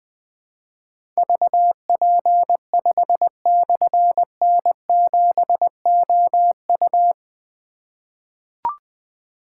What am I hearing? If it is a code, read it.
Morse code, heard as a keyed tone, about 20 wpm, 700 Hz. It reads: VP5/N7OU
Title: Difficult Call Signs 20wpm